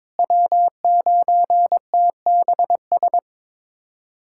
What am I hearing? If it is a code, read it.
Morse code, heard as a keyed tone, about 22 wpm, 700 Hz. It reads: W9TBS
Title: USA Callsign DE Callsign - 5 Characters - 22wpm